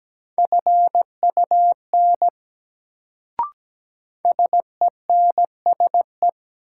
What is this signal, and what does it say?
Morse code, heard as a keyed tone, about 17 wpm, 700 Hz. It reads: FUN SENSE
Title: Sentences from Top 700 Words - New Word Drill - 17wpm